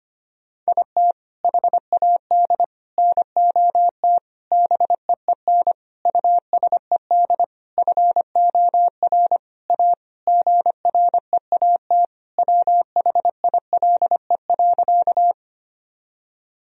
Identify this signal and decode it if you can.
Morse code, heard as a keyed tone, about 25 wpm, 700 Hz. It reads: IT HAD NOT BEEN USED FOR A GREAT WHILE.
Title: Sentences from Top 1000 Words - Review Part 1 - 25wpm